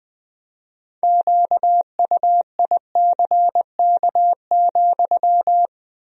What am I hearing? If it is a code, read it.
Morse code, heard as a keyed tone, about 21 wpm, 700 Hz. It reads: QUICK,
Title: 4x Word-Spacing - Sentences from Top 500 Words - 20wpm